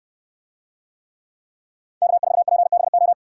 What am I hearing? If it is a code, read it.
Morse code, heard as a keyed tone, about 68 wpm, 700 Hz. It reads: K4PBO